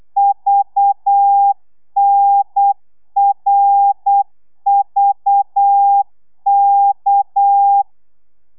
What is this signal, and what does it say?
Morse code, heard as a keyed tone, about 8 wpm, 800 Hz. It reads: VNRVK
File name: telegrafia / i2tfj / Lez6 35